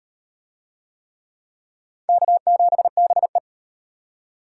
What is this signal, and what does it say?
Morse code, heard as a keyed tone, about 33 wpm, 700 Hz. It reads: K7BE